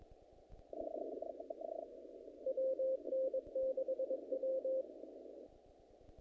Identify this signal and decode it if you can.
Morse code, heard as a keyed tone, about 22 wpm, 530 Hz. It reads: WN6W
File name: contests / naqp-cw-k5tr-op-k5pi- -2000